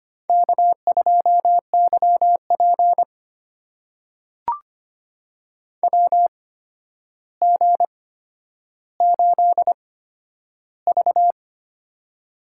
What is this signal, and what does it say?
Morse code, heard as a keyed tone, about 25 wpm, spaced as for 11 wpm, 700 Hz. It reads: K2YP WG8V
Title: USA Call Signs - 4 Characters - 8x Character Spacing - 25wpm